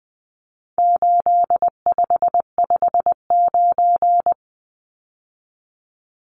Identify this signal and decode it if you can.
Morse code, heard as a keyed tone, about 20 wpm, 700 Hz. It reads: 8559
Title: Numbers - 4-Digit 20wpm